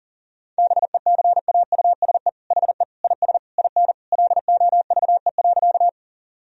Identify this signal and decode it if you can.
Morse code, heard as a keyed tone, about 40 wpm, 700 Hz. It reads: BECAUSE HE IS IN LOVE.
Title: Sentences from Top 1000 Words - Review Part 3 - 40wpm